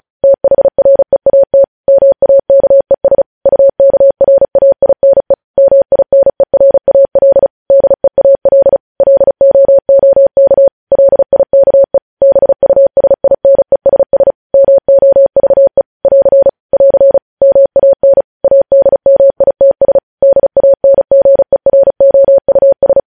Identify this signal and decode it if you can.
Morse code, heard as a keyed tone, about 36 wpm, 550 Hz. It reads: THREAT MAKES UKRAINE MINERAL DEAL LOOK LIKE BUSINESS MOVE <AR> <AR> MAN ADMITS DANGEROUS
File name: podcast / News headlines-1x-35x35